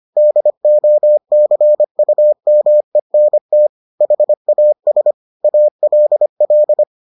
Morse code, heard as a keyed tone, about 25 wpm, 600 Hz. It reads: DOCUMENT HAS ALL